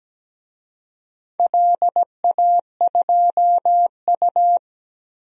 Morse code, heard as a keyed tone, about 17 wpm, 700 Hz. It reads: LA2U
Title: DXCC-100 Call Signs - 4 Characters - Repeated 2x - 17wpm